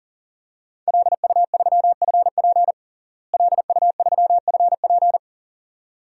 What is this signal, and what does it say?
Morse code, heard as a keyed tone, about 40 wpm, 700 Hz. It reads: LU3FP LU3FP
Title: DXCC-100 Call Signs - 5 Characters - Repeated 2x - 40wpm